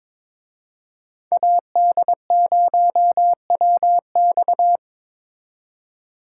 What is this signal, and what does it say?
Morse code, heard as a keyed tone, about 25 wpm, 700 Hz. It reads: AD0WX
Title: Speed Racing - USA 2x2 Call Signs 22wpm